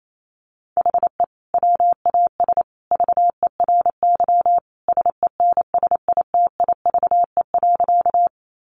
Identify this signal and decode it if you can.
Morse code, heard as a keyed tone, about 28 wpm, 700 Hz. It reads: HE WAS VERY SENSITIVE.
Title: Sentences from Top 1400 Words 28wpm